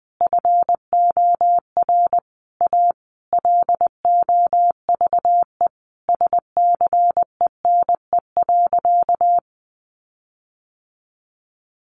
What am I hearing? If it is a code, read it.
Morse code, heard as a keyed tone, about 20 wpm, 700 Hz. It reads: FOR A LOVE SCENE.
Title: Sentences from Top 1300 Words 20wpm